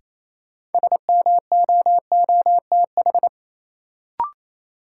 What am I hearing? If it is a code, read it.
Morse code, heard as a keyed tone, about 27 wpm, 700 Hz. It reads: SMOOTH
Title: Sentences from Top 1400 Words - New Word Drill - 28wpm